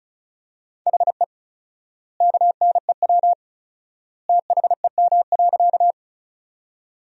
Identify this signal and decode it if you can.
Morse code, heard as a keyed tone, about 34 wpm, 700 Hz. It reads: HE KNEW THEM.
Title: 4x Word-Spacing - Sentences from Top 400 Words - 35wpm